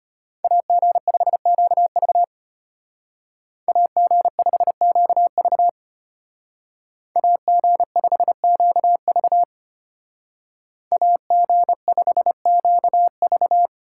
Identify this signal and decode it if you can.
Morse code, heard as a keyed tone, about 31 wpm, 700 Hz. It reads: AG5QV AG5QV AG5QV AG5QV
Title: Speed Racing - USA Call Signs 25wpm